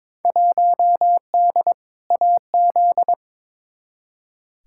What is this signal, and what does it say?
Morse code, heard as a keyed tone, about 22 wpm, 700 Hz. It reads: 1D AZ
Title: Field Day Exchanges - 22wpm